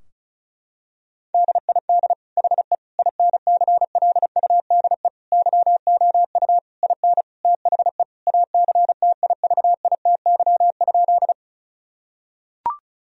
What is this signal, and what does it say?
Morse code, heard as a keyed tone, about 36 wpm, 700 Hz. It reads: DID HE INCLUDE YOU IN THE ACTIVITY?
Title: Sentences from Top 1000 Words - Review Part 4 - 35wpm